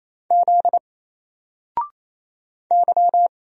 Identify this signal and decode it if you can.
Morse code, heard as a keyed tone, about 27 wpm, 700 Hz. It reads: Z Y
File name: Single-Letter-Number-28wpm